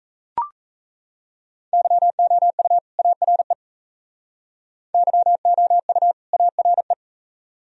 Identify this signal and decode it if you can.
Morse code, heard as a keyed tone, about 38 wpm, 700 Hz. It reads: YOU ARE YOU ARE